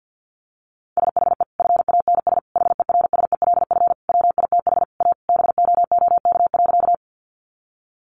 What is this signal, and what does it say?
Morse code, heard as a keyed tone, about 50 wpm, 700 Hz. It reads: SHE FANS HERSELF WITH A BOOK.
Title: Easy Phrases 50wpm